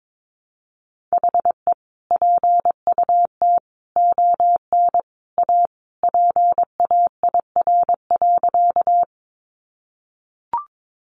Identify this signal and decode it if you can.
Morse code, heard as a keyed tone, about 22 wpm, 700 Hz. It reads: HE PUT ON A PAIR.
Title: Sentences from Top 1200 Words 22wpm